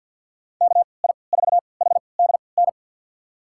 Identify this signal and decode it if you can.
Morse code, heard as a keyed tone, about 51 wpm, spaced as for 28 wpm, 700 Hz. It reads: KI4HDN
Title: USA Call Signs - 3x Character Spacing - 50wpm